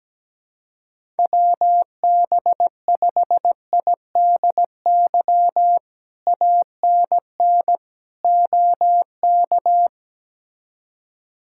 Morse code, heard as a keyed tone, about 17 wpm, 700 Hz. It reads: WB5IDY ANN OK
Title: K1USN SST QSO Practice - 17wpm